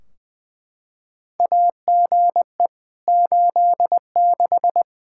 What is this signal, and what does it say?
Morse code, heard as a keyed tone, about 20 wpm, 700 Hz. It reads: AGE 86
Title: K1USN SST QSO Practice 20wpm